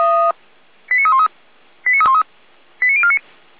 Five-tone selective call (ZVEI), heard as quick runs of five tones, tones of about 70 ms. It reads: 89313 89313 89049